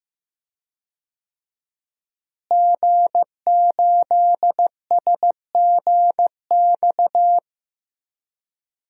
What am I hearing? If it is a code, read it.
Morse code, heard as a keyed tone, about 15 wpm, 700 Hz. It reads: G8SGX
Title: DXCC-100 Call Signs - 5 Characters - Repeated 3x - 15wpm